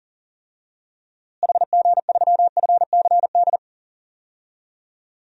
Morse code, heard as a keyed tone, about 40 wpm, 700 Hz. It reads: HG3FCD
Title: DXCC-100 Call Signs - 6 Characters - Repeated 2x - 40wpm